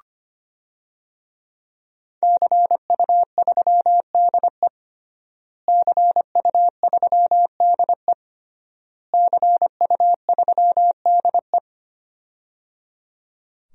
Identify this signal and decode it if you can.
Morse code, heard as a keyed tone, about 25 wpm, 700 Hz. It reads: CU3DE CU3DE CU3DE
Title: DXCC-100 Call Signs - 5 Characters - Repeated 3x - 25wpm